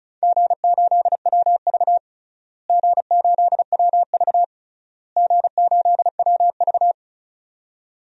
Morse code, heard as a keyed tone, about 34 wpm, 700 Hz. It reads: G8WV G8WV G8WV
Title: DXCC-100 Call Signs - 4 Characters - Repeated 3x - 35wpm